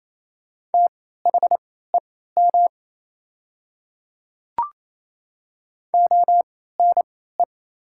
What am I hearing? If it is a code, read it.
Morse code, heard as a keyed tone, about 28 wpm, spaced as for 16 wpm, 690 Hz. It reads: THEM ONE
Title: 3x Character Spacing - Top 100 Words - 28wpm